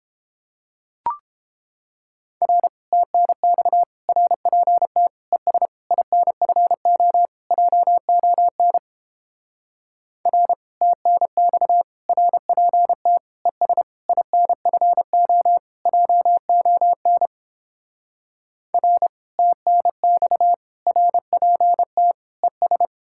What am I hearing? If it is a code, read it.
Morse code, heard as a keyed tone, about 29 wpm, 700 Hz. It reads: R TNX RPT ES INFO JON R TNX RPT ES INFO JON R TNX RPT ES